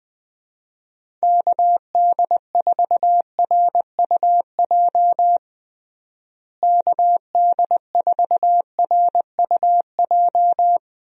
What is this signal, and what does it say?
Morse code, heard as a keyed tone, about 20 wpm, 700 Hz. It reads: KD4RUJ KD4RUJ
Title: USA Call Signs - 6 Characters - Repeated 2x - 20wpm